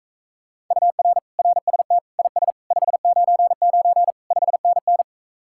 Morse code, heard as a keyed tone, about 45 wpm, 700 Hz. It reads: UR RST IS 599 5NN